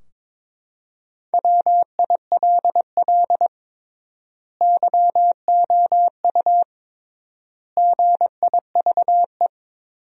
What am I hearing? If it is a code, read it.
Morse code, heard as a keyed tone, about 22 wpm, 700 Hz. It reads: WILL YOU GIVE
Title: 3x Word-Spacing - Sentences from Top 200 Words - 22wpm